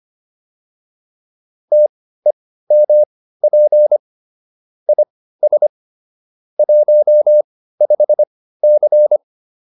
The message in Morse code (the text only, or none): TEMP IS 15C